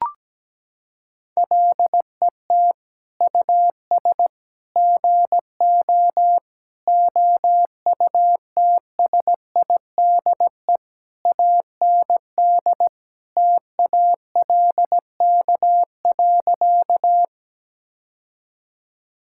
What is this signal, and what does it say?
Morse code, heard as a keyed tone, about 17 wpm, 700 Hz. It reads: LET US GO OUTSIDE AND TALK.
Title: Sentences from Top 1000 Words - Review Part 6 - 17wpm